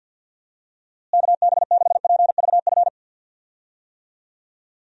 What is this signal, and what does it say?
Morse code, heard as a keyed tone, about 53 wpm, 700 Hz. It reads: KB6PVF
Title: USA Call Signs - 6 Characters - Repeated 2x - 50wpm